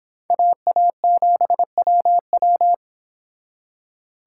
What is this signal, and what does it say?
Morse code, heard as a keyed tone, about 26 wpm, 700 Hz. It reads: AA7WW